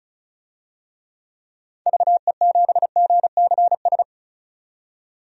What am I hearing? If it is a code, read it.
Morse code, heard as a keyed tone, about 36 wpm, 700 Hz. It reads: VE7GCS
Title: DXCC-100 Call Signs - 6 Characters - 35wpm